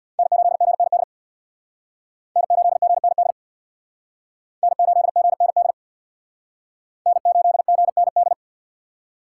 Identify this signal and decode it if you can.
Morse code, heard as a keyed tone, about 62 wpm, 700 Hz. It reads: N8GND N8GND N8GND N8GND
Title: Speed Racing - USA 1x3 Call Signs 50wpm